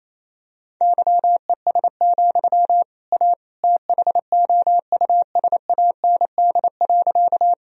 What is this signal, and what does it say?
Morse code, heard as a keyed tone, about 28 wpm, 700 Hz. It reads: YES, A THOUSAND.
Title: Sentences from Top 1000 Words - Review Part 3 - 28wpm